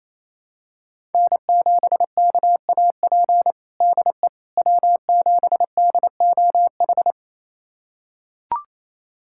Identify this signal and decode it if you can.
Morse code, heard as a keyed tone, about 28 wpm, 700 Hz. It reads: N7KAP DE W7DOH
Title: USA Callsign DE Callsign - 5 Characters - 28wpm